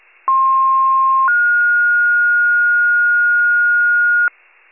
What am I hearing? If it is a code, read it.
Two-tone page, heard as a one-second tone, then a three-second tone, about 1060 then 1430 Hz, with noise. It is heard from 0.3 to 4.3 s.